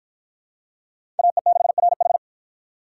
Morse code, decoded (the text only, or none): AE6RH